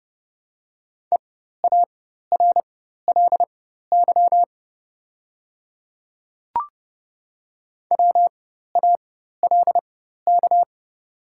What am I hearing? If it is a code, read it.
Morse code, heard as a keyed tone, about 31 wpm, spaced as for 14 wpm, 700 Hz. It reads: EARLY WALK
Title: 4x Character Spacing - Top 500 Words - 30wpm